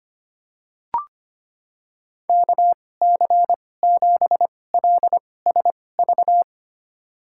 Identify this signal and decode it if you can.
Morse code, heard as a keyed tone, about 25 wpm, spaced as for 18 wpm, 700 Hz. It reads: KC7LSV
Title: USA Call Signs - 6 Characters - 2x Character Spacing - 25wpm